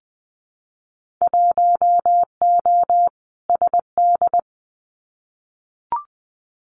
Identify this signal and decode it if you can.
Morse code, heard as a keyed tone, about 20 wpm, 700 Hz. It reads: 1O SD
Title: Winter Field Day Exchanges Encore 20wpm